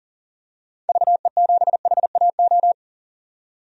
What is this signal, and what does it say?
Morse code, heard as a keyed tone, about 40 wpm, 700 Hz. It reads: VE7HAO
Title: DXCC-100 Call Signs - 6 Characters - 40wpm